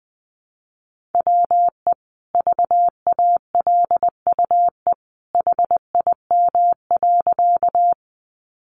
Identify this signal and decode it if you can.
Morse code, heard as a keyed tone, about 20 wpm, 700 Hz. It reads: WE VALUE HIM.
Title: Sentences from Top 600 Words 20wpm